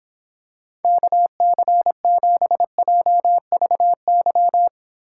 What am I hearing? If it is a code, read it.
Morse code, heard as a keyed tone, about 26 wpm, 700 Hz. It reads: KC7JVY